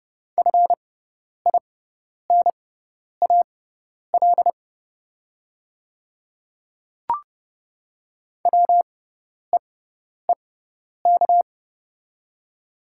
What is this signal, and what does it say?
Morse code, heard as a keyed tone, about 30 wpm, spaced as for 10 wpm, 690 Hz. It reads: FINAL WEEK
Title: 6x Character Spacing - Top 500 Words - 30wpm